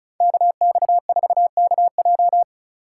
Morse code, heard as a keyed tone, about 36 wpm, 700 Hz. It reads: KX4KJ
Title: USA Call Signs - 2x Character Spacing - 35wpm